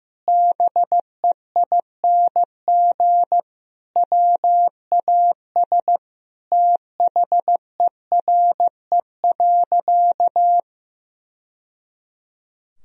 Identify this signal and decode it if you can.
Morse code, heard as a keyed tone, about 15 wpm, 700 Hz. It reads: BEING WAS THERE.
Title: Sentences from Top 1000 Words - Review Part 4 - 15wpm